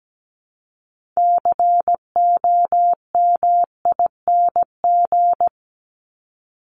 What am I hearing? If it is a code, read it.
Morse code, heard as a keyed tone, about 17 wpm, 700 Hz. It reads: COMING
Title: ING Suffix - Top 100 Words with Base Words 17wpm